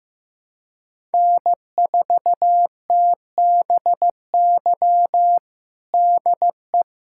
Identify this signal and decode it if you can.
Morse code, heard as a keyed tone, about 15 wpm, 700 Hz. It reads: N4TBY DE
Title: USA Callsign DE Callsign - 5 Characters - 15wpm